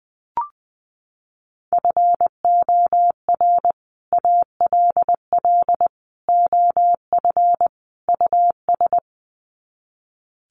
Morse code, heard as a keyed tone, about 20 wpm, 700 Hz. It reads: FOR ALL OF US
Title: Sets of 4 Words 20wpm